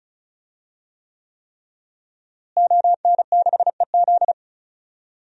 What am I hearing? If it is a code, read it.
Morse code, heard as a keyed tone, about 36 wpm, 700 Hz. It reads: ON6EZ
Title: DXCC-100 Call Signs - 5 Characters - Repeated 3x - 35wpm